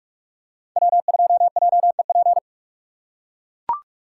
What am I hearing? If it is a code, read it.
Morse code, heard as a keyed tone, about 44 wpm, 700 Hz. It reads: W2JEP